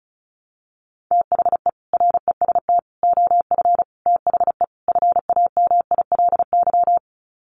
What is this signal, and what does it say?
Morse code, heard as a keyed tone, about 34 wpm, 700 Hz. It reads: THE REST OF THE FAMILY
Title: Sets of 5 Words Encore2 35wpm